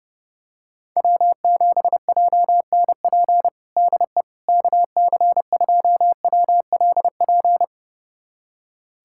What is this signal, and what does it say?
Morse code, heard as a keyed tone, about 30 wpm, 700 Hz. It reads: W7JNP DE KC2WLP
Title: USA Callsign DE Callsign 30wpm